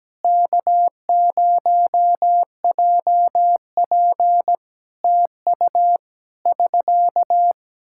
Morse code, heard as a keyed tone, about 17 wpm, 700 Hz. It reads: K0JP TU <SK>
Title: Practice QSO - Exchange 4 - Ending - 17wpm